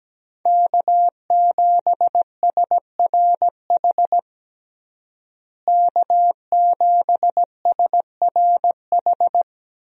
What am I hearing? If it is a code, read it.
Morse code, heard as a keyed tone, about 17 wpm, 700 Hz. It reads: K7SRH K7SRH
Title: USA Call Signs - 5 Characters - Repeated 3x - 17wpm